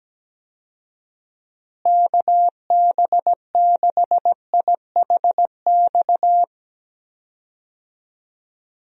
Morse code, heard as a keyed tone, about 17 wpm, 700 Hz. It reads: KB6IHX
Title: USA Call Signs - 6 Characters - Repeated 3x - 17wpm